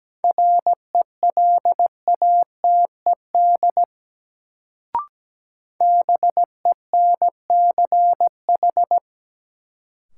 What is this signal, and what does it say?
Morse code, heard as a keyed tone, about 17 wpm, 700 Hz. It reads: RELATED BENCH
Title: Sentences from Top 1400 Words - New Word Drill Alt - 17wpm